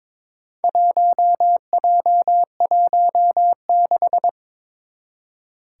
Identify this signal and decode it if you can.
Morse code, heard as a keyed tone, about 22 wpm, 700 Hz. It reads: 1J16
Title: Four Challenging Letter-Number Combinations 22wpm